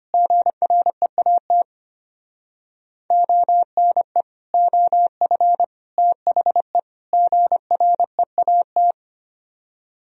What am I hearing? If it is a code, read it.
Morse code, heard as a keyed tone, about 32 wpm, 700 Hz. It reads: GREAT ONE OF THE GREAT